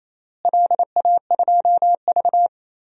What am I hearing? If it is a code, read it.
Morse code, heard as a keyed tone, about 28 wpm, 700 Hz. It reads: LA2V
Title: DXCC-100 Call Signs - 4 Characters - Repeated 3x - 28wpm